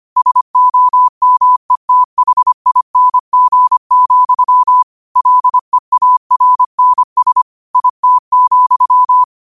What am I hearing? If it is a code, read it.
Morse code, heard as a keyed tone, about 25 wpm, 1000 Hz. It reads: SOMETHING, LEARNS IT,
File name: 25wpm